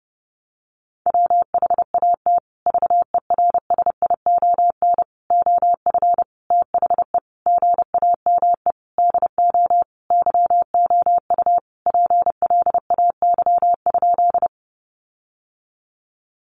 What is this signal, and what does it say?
Morse code, heard as a keyed tone, about 30 wpm, 700 Hz. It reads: WHAT VERSION OF THE GAME DO YOU PLAY?
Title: Sentences from Top 1400 Words 30wpm